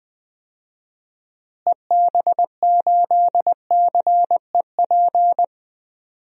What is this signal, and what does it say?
Morse code, heard as a keyed tone, about 20 wpm, 700 Hz. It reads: EB8CEP
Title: DXCC-100 Call Signs - 6 Characters - Repeated 3x - 20wpm